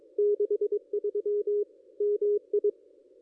Morse code, heard as a keyed tone, about 24 wpm, 440 Hz. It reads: 63 MI
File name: contests / ss-cw-k5tr-op-n5rz -1400